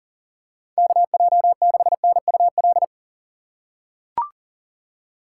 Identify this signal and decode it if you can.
Morse code, heard as a keyed tone, about 40 wpm, 700 Hz. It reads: KJ6NUL
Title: USA Call Signs - 6 Characters - 2x Character Spacing - 40wpm